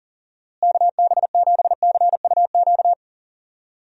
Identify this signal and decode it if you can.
Morse code, heard as a keyed tone, about 40 wpm, 700 Hz. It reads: KB7CUQ
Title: USA Call Signs - 6 Characters - Repeated 3x - 40wpm